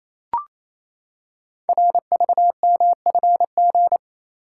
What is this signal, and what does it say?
Morse code, heard as a keyed tone, about 29 wpm, 700 Hz. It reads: RVMFG
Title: Code Groups - 5 Letter:Numbers - 28wpm